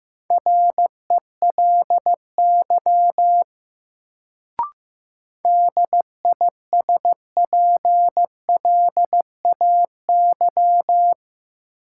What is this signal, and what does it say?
Morse code, heard as a keyed tone, about 15 wpm, 700 Hz. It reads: RELY DISPLAY
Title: Sentences from Top 1200 Words - New Word Drill Alt - 15wpm